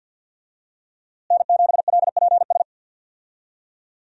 Morse code, heard as a keyed tone, about 49 wpm, 690 Hz. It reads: N7LPS
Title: USA Call Signs - 5 Characters - Repeated 3x - 50wpm